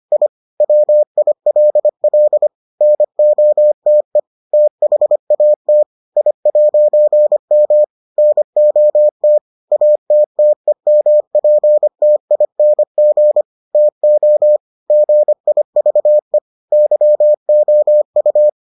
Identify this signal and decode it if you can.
Morse code, heard as a keyed tone, about 25 wpm, 600 Hz. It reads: I WILL NOTE THAT I'M NOT ATTEMPTING TO GIVE YOU